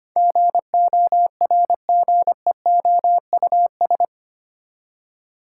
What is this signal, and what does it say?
Morse code, heard as a keyed tone, about 25 wpm, 700 Hz. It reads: GORGEOUS
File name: Eight Letter Words - 25wpm